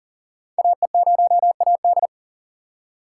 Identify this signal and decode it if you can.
Morse code, heard as a keyed tone, about 40 wpm, 690 Hz. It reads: AE0AD
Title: Code Groups - 5 Letter:Numbers - 40wpm